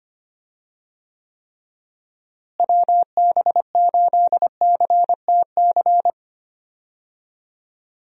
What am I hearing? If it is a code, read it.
Morse code, heard as a keyed tone, about 25 wpm, 700 Hz. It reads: WB8CTC
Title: USA Call Signs - 6 Characters - Repeated 3x - 25wpm